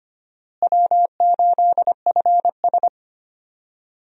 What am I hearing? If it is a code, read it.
Morse code, heard as a keyed tone, about 25 wpm, 700 Hz. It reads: W8FS
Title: USA Call Signs - 4 Characters - Repeated 3x - 25wpm